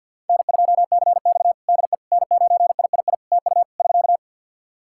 Morse code, heard as a keyed tone, about 52 wpm, 700 Hz. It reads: N2CX DE N9III TU <SK>
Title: K1USN SST QSO Practice 50wpm